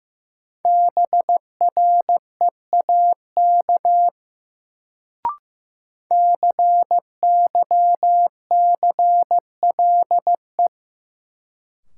Morse code, heard as a keyed tone, about 15 wpm, 700 Hz. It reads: BREAK CYCLE
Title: Sentences from Top 900 Words - New Word Drill - 15wpm